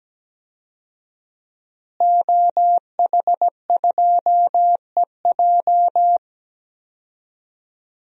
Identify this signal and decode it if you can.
Morse code, heard as a keyed tone, about 17 wpm, 700 Hz. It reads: OH2EJ
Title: DXCC-100 Call Signs - 5 Characters - Repeated 3x - 17wpm